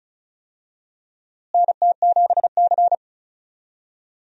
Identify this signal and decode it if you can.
Morse code, heard as a keyed tone, about 36 wpm, 700 Hz. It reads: NT7C
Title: USA Call Signs - 4 Characters - Repeated 3x - 35wpm